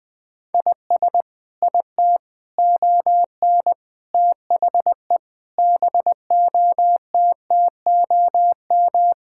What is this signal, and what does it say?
Morse code, heard as a keyed tone, about 21 wpm, 700 Hz. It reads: IS IT ON THE BOTTOM
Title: Sentences from Top 1000 Words - Review Part 6 - 20wpm